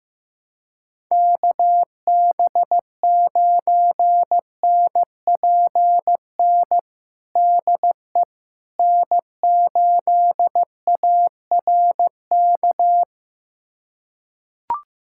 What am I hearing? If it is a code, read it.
Morse code, heard as a keyed tone, about 15 wpm, 700 Hz. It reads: KB9NPN DE N8ARK
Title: USA Callsign DE Callsign 15wpm